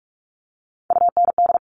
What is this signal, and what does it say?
Morse code, heard as a keyed tone, about 46 wpm, 700 Hz. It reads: UND
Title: Top Three Letter Combinations in Words 45wpm